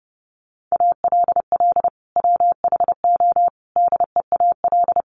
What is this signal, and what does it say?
Morse code, heard as a keyed tone, about 30 wpm, 700 Hz. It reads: ALL WHO DEAL
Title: Sentences from Top 1400 Words 30wpm